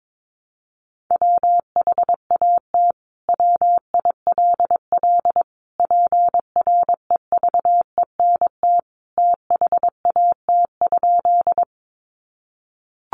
Morse code, heard as a keyed tone, about 22 wpm, 700 Hz. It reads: WHAT WILL PREVENT THAT?
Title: Sentences from Top 1100 Words 22wpm